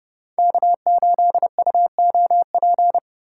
Morse code, heard as a keyed tone, about 28 wpm, 700 Hz. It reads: K8UOP